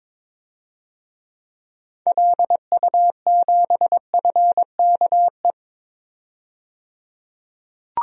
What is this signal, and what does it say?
Morse code, heard as a keyed tone, about 22 wpm, 700 Hz. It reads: LU7FKE
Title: DXCC-100 Call Signs - 6 Characters - 22wpm